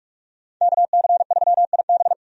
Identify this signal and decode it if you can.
Morse code, heard as a keyed tone, about 46 wpm, 700 Hz. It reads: KC3IB
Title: USA Call Signs - 5 Characters - Repeated 3x - 45wpm